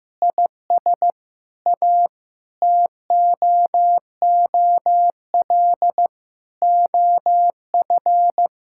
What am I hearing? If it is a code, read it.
Morse code, heard as a keyed tone, about 15 wpm, 700 Hz. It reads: IS A TOOL OF
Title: Sentences from Top 1000 Words - Review Part 6 - 15wpm